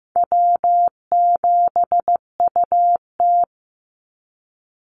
Morse code, heard as a keyed tone, about 15 wpm, 700 Hz. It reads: W7UT
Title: Easy Call Signs Repeated 2x - 15wpm